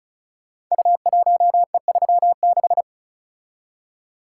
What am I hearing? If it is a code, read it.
Morse code, heard as a keyed tone, about 36 wpm, 690 Hz. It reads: U1E36
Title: Code Groups - 5 Letter:Numbers - 35wpm